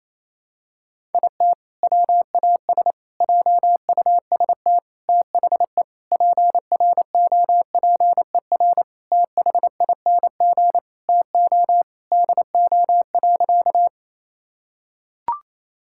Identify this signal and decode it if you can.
Morse code, heard as a keyed tone, about 28 wpm, 700 Hz. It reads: IT WAS JUST THE PROPER THING TO DO.
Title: Sentences from Top 1000 Words - Review Part 5 - 28wpm